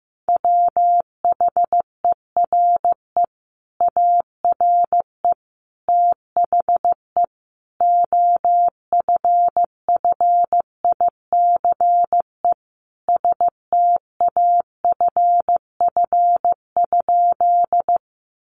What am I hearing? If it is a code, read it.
Morse code, heard as a keyed tone, about 15 wpm, 700 Hz. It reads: WHERE ARE THE OFFICE STAFF?
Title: Sentences from Top 1200 Words 15wpm